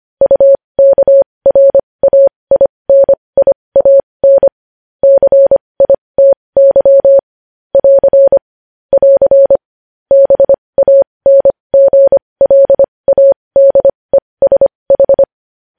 Morse code, heard as a keyed tone, about 25 wpm, spaced as for 20 wpm, 550 Hz. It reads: UKRAINIAN CITY <AR> <AR> BANGLADESH
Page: news-headlines-in-morse-code-at-20-wpm